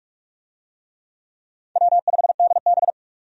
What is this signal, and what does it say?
Morse code, heard as a keyed tone, about 43 wpm, 700 Hz. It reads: W5DB
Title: USA Call Signs - 4 Characters - 45wpm